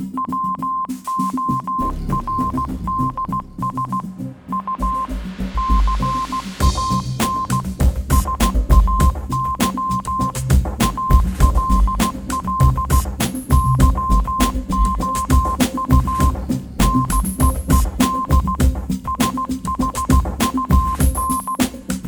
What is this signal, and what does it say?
Morse code, heard as a keyed tone, about 16 wpm, 1050 Hz. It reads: WORDS U CAN SAY ALL YOU NEED SHUN
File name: Kirkoid / Kirkoid - Small Stuff (Morse Mix)